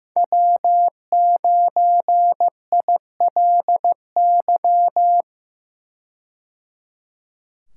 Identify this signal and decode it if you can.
Morse code, heard as a keyed tone, about 15 wpm, 700 Hz. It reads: W9ILY
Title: K1USN SST QSO Practice - 15wpm